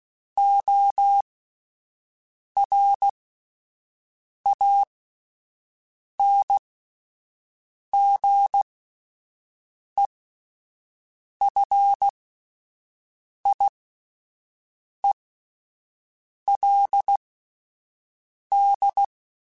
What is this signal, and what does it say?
Morse code, heard as a keyed tone, about 16 wpm, spaced as for 5 wpm, 800 Hz. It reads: ORANGEFIELD